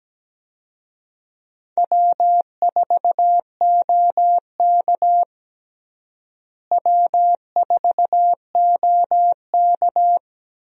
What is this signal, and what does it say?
Morse code, heard as a keyed tone, about 17 wpm, 700 Hz. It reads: W4OK W4OK
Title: USA Call Signs - 4 Characters - Repeated 3x - 17wpm